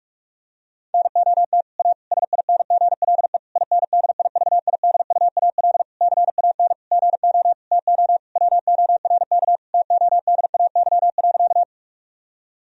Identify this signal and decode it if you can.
Morse code, heard as a keyed tone, about 46 wpm, 700 Hz. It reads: NOT A SINGLE INDIVIDUAL CAN GO TO WORK TODAY.
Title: Sentences from Top 1000 Words - Review Part 4 - 45wpm